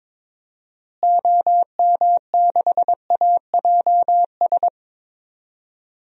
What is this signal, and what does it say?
Morse code, heard as a keyed tone, about 22 wpm, 700 Hz. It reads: OM6AJS
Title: DXCC-100 Call Signs - 6 Characters - Repeated 2x - 22wpm